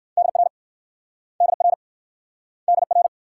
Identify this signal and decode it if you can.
Morse code, heard as a keyed tone, about 61 wpm, 700 Hz. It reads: DR DR DR